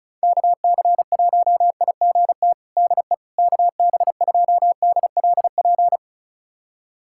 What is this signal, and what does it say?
Morse code, heard as a keyed tone, about 36 wpm, 700 Hz. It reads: KC1IGT DE KB2DLP
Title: USA Callsign DE Callsign - 6 Characters - 35wpm